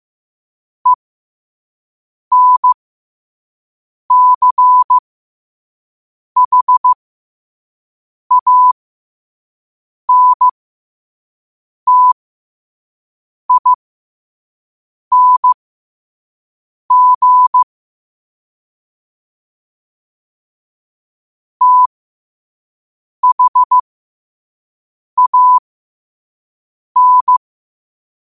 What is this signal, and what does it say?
Morse code, heard as a keyed tone, about 15 wpm, spaced as for 5 wpm, 1000 Hz. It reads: ENCHANTING THAN
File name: qotdfiles / 5wpm